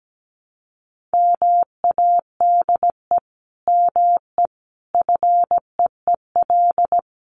Morse code, heard as a keyed tone, about 17 wpm, 700 Hz. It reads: MADE ME FEEL
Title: Sentences from Top 400 Words 17wpm